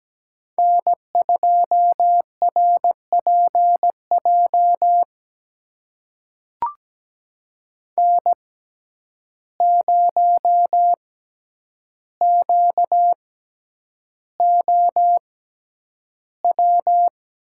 Morse code, heard as a keyed tone, about 17 wpm, spaced as for 9 wpm, 700 Hz. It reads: N2RPJ N0QOW
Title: USA Call Signs - 5 Characters - 6x Character Spacing - 17wpm